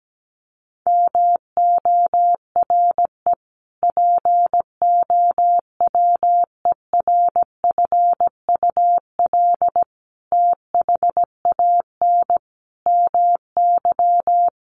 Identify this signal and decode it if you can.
Morse code, heard as a keyed tone, about 17 wpm, 700 Hz. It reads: MORE POWERFUL THAN MY
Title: Sentences from Top 1300 Words 17wpm